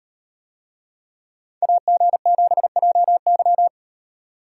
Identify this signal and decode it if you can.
Morse code, heard as a keyed tone, about 35 wpm, 700 Hz. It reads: AG7JY